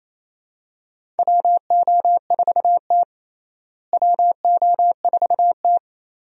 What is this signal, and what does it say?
Morse code, heard as a keyed tone, about 28 wpm, 700 Hz. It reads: WO4T WO4T
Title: USA Call Signs - 4 Characters - Repeated 3x - 28wpm